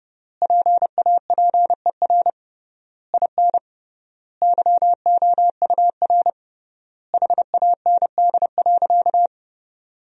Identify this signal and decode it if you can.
Morse code, heard as a keyed tone, about 30 wpm, 700 Hz. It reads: PAPER IN YOUR HAND.
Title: 3x Word-Spacing - Sentences from Top 400 Words - 30wpm